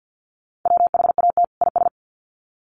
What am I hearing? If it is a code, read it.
Morse code, heard as a keyed tone, about 50 wpm, 700 Hz. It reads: WHAT IS
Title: Sets of 2 Words 50wpm